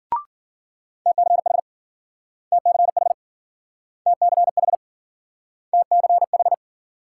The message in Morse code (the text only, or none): TCH TCH TCH TCH